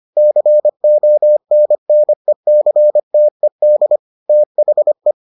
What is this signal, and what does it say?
Morse code, heard as a keyed tone, about 25 wpm, 600 Hz. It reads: CONNECTED THE